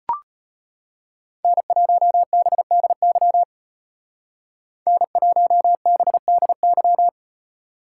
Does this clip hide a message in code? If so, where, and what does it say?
Yes: Morse code from 1.4 to 7.1 s, heard as a keyed tone, about 34 wpm, 700 Hz. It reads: N1BDY N1BDY